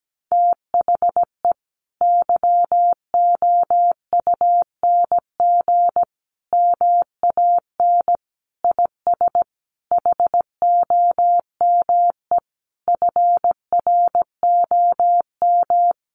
Morse code, heard as a keyed tone, about 17 wpm, 700 Hz. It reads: THE YOUNG MAN IS HOME FROM
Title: Sentences from Top 1300 Words 17wpm